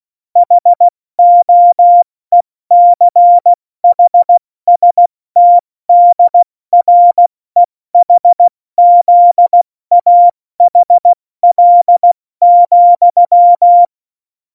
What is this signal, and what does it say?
Morse code, heard as a keyed tone, about 16 wpm, 700 Hz. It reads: HOECHSTDREHZAHL,
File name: Morsekurs / Witze / Witz 26. Januar 2021, 70 F80